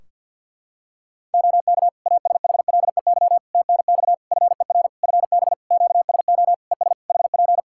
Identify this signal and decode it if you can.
Morse code, heard as a keyed tone, about 53 wpm, 700 Hz. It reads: OK ASHLEY TNX FER FB QSO ES HP
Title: Practice QSO - Exchange 4 - Ending - 50wpm